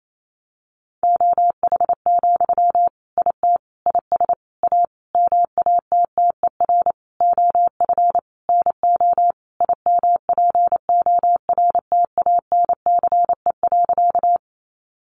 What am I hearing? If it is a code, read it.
Morse code, heard as a keyed tone, about 28 wpm, 700 Hz. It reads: OH, IT IS A MATTER OF NO IMPORTANCE.
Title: Sentences from Top 1200 Words 28wpm